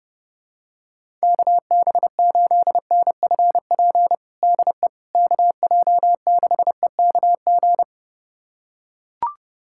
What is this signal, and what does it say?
Morse code, heard as a keyed tone, about 30 wpm, 700 Hz. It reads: KB8NFP DE KJ6EKG
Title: USA Callsign DE Callsign - 6 Characters - 30wpm